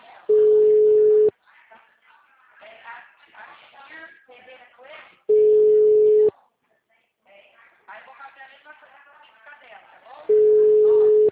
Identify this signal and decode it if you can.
Ringback tone (Europe, 425 Hz), with speech.
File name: out- -2012- - - .4500